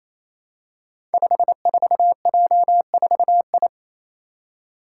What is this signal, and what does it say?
Morse code, heard as a keyed tone, about 27 wpm, 700 Hz. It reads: 54J4I
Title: Five Challenging Letter-Number Combinations 28wpm